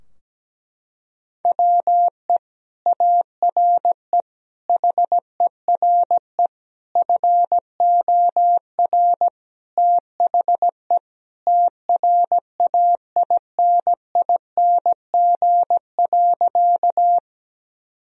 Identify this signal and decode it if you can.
Morse code, heard as a keyed tone, about 17 wpm, 700 Hz. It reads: WE ARE HERE FOR THE TRAINING.
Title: Sentences from Top 1000 Words - Review Part 4 - 17wpm